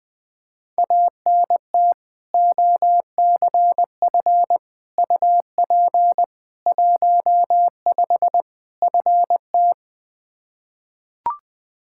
Morse code, heard as a keyed tone, about 20 wpm, 700 Hz. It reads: ANT OCF UP 15 FT
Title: Practice QSO - Exchange 2 - Info - 20wpm